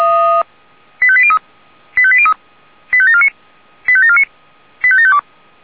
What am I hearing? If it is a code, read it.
Five-tone selective call (ZVEI), heard as quick runs of five tones, tones of about 70 ms. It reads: 85803 85803 85749 85749 85752